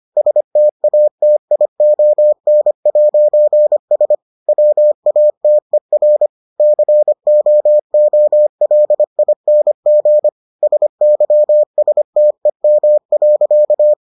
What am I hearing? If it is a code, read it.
Morse code, heard as a keyed tone, about 25 wpm, 600 Hz. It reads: STATION'S WATER COOLING SYSTEM.